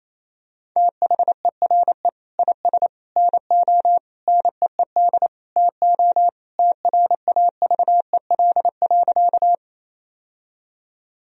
Morse code, heard as a keyed tone, about 27 wpm, 700 Hz. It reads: THERE IS NO NEED TO TRAVEL.
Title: Sentences from Top 1000 Words - Review Part 2 - 28wpm